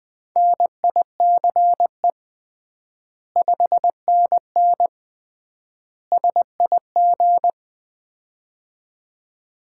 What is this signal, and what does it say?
Morse code, heard as a keyed tone, about 21 wpm, 700 Hz. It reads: NICE 5NN SIG
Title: Common QSO Phrases - 3x Word-Spacing - 20wpm